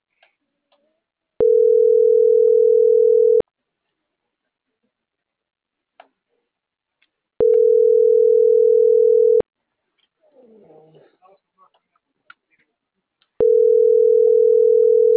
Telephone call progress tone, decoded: ringback tone

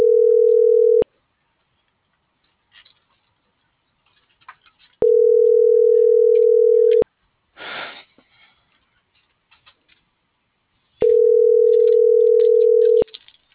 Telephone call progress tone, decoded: ringback tone

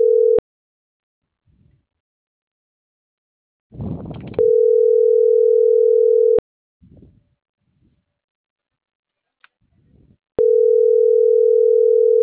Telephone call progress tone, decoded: ringback tone

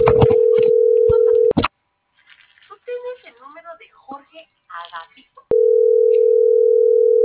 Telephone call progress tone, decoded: ringback tone

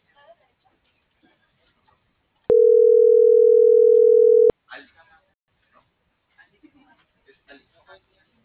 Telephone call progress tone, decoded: ringback tone

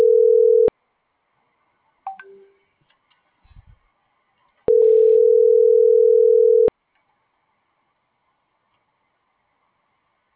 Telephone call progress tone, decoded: ringback tone

